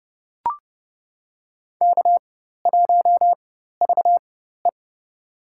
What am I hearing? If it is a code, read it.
Morse code, heard as a keyed tone, about 29 wpm, spaced as for 14 wpm, 700 Hz. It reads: K1VE
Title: USA Call Signs - 4 Characters - 4x Character Spacing - 30wpm